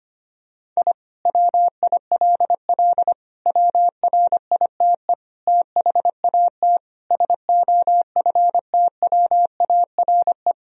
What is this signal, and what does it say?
Morse code, heard as a keyed tone, about 25 wpm, 700 Hz. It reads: I WILL WRITE THAT SOFTWARE
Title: Sentences from Top 1000 Words - Review Part 3 - 25wpm